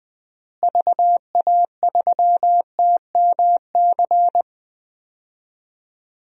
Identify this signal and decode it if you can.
Morse code, heard as a keyed tone, about 20 wpm, 700 Hz. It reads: VA3TMC
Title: DXCC-100 Call Signs - 6 Characters - Repeated 2x - 20wpm